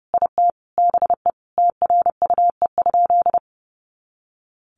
Morse code, heard as a keyed tone, about 30 wpm, 700 Hz. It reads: IT BE TRUE?
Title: Sentences from Top 400 Words 30wpm